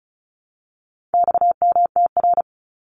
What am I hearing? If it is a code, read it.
Morse code, heard as a keyed tone, about 34 wpm, 700 Hz. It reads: XMTR
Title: Abbreviations 35wpm